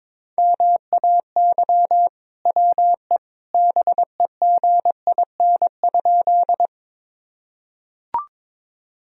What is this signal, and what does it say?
Morse code, heard as a keyed tone, about 21 wpm, 700 Hz. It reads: MAY WE BEGIN?
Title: Sentences from Top 1000 Words - Review Part 1 - 22wpm